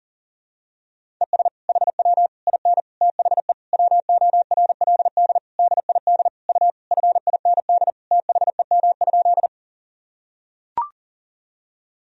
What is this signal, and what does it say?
Morse code, heard as a keyed tone, about 40 wpm, 700 Hz. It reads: ES HW IN THE WORLD DID U FIND THEM?
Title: Sentences from Top 200 Words and Abbreviations 40wpm